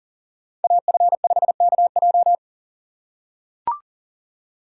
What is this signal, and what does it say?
Morse code, heard as a keyed tone, about 37 wpm, 700 Hz. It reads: AF5KJ
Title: USA Call Signs - 5 Characters - 4x Character Spacing - 40wpm